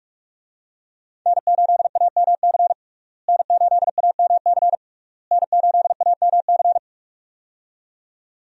Morse code, heard as a keyed tone, about 46 wpm, 700 Hz. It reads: N8AMC N8AMC N8AMC
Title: USA Call Signs - 5 Characters - Repeated 3x - 45wpm